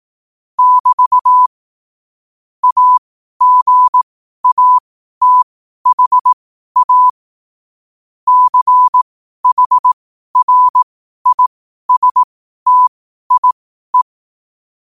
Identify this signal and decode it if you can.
Morse code, heard as a keyed tone, about 18 wpm, spaced as for 12 wpm, 1000 Hz. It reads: <BT> AGATHA CHRISTIE
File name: qotdfiles / 12wpm